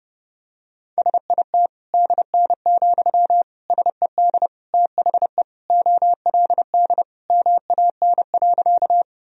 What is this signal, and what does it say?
Morse code, heard as a keyed tone, about 32 wpm, 700 Hz. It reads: SIT DN, SED THE OLD MAN.
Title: Sentences from Top 500 Words and Abbreviations 30wpm